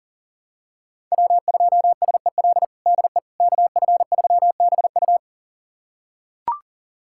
Morse code, heard as a keyed tone, about 37 wpm, 700 Hz. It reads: W2SEL DE KF3BU
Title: USA Callsign DE Callsign - 5 Characters - 40wpm